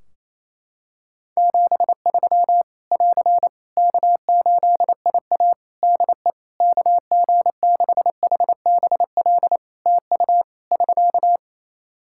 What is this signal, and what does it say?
Morse code, heard as a keyed tone, about 28 wpm, 700 Hz. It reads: 73 <AR> K8IA DE KG6HBL TU <SK>
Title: Practice QSO - Exchange 4 - Ending - 28wpm